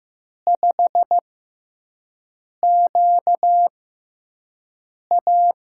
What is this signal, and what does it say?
Morse code, heard as a keyed tone, about 15 wpm, spaced as for 5 wpm, 700 Hz. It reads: 5QA
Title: USA Call Signs - 6x Character Spacing - 15wpm